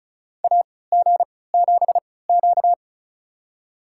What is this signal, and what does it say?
Morse code, heard as a keyed tone, about 34 wpm, spaced as for 20 wpm, 700 Hz. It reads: AG7Q
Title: USA Call Signs - 4 Characters - 3x Character Spacing - 35wpm